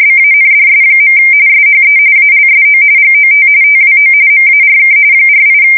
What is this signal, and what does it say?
RTTY, about 45.45 baud. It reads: BANDS. SHE ALSO GOT TWO MARK 48
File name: fsk170 / battleshipnj